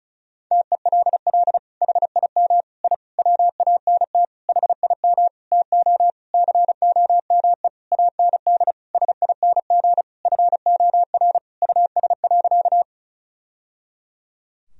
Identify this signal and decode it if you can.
Morse code, heard as a keyed tone, about 36 wpm, 700 Hz. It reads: TELL HIM I WANT HIM TO COME AND SING FOR US.
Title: Sentences from Top 1000 Words - Review Part 2 - 35wpm